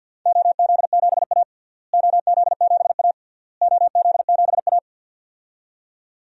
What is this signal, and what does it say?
Morse code, heard as a keyed tone, about 53 wpm, 700 Hz. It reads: OZ7A OZ7A OZ7A
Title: DXCC-100 Call Signs - 4 Characters - Repeated 3x - 50wpm